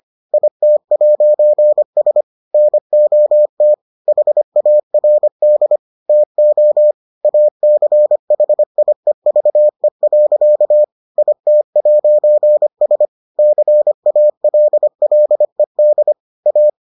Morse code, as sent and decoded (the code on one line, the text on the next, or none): .. - .----. ... / -. --- - / .... .- .-. -.. / - --- / .- -.-. .... .. ...- . .-.-.- / .. - .----. ... / -.-. .- .-.. .-.. -.. / .-
IT'S NOT HARD TO ACHIEVE. IT'S CALLED A